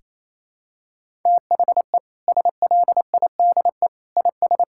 Morse code, heard as a keyed tone, about 28 wpm, 700 Hz. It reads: THE SLIDE IS
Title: Sentences from Top 1600 Words 28wpm